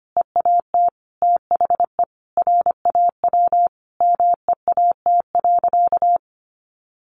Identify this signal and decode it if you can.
Morse code, heard as a keyed tone, about 25 wpm, 700 Hz. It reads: EAT THE RAW MEAT.
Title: Sentences from Top 1500 Words 25wpm